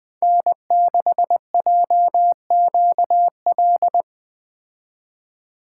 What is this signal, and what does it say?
Morse code, heard as a keyed tone, about 20 wpm, 700 Hz. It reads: N6JQL
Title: Speed Racing - USA 1x3 Call Signs 17wpm